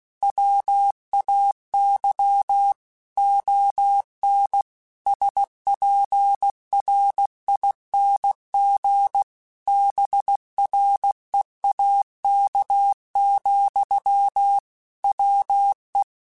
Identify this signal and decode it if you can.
Morse code, heard as a keyed tone, about 16 wpm, 800 Hz. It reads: WAY ON SPRING BREAK, WE